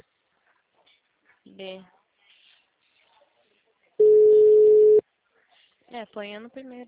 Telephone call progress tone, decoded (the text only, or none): ringback tone